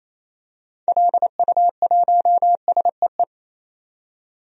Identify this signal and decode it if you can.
Morse code, heard as a keyed tone, about 28 wpm, 700 Hz. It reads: LU1SEE
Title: DXCC-100 Call Signs - 6 Characters - 28wpm